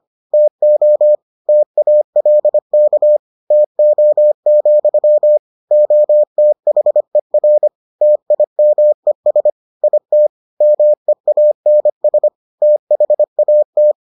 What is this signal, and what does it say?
Morse code, heard as a keyed tone, about 25 wpm, 600 Hz. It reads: TO TALK TO, OTHER TIMES IT MEANS THAT